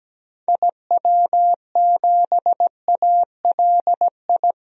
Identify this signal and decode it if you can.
Morse code, heard as a keyed tone, about 17 wpm, 700 Hz. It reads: IW7ALI
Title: DXCC-100 Call Signs - 6 Characters - Repeated 3x - 17wpm